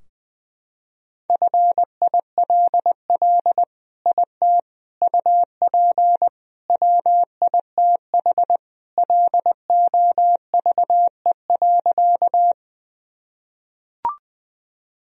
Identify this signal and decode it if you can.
Morse code, heard as a keyed tone, about 21 wpm, 700 Hz. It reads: FILL IT UP WITH LOVE.
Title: Sentences from Top 1000 Words - Review Part 3 - 20wpm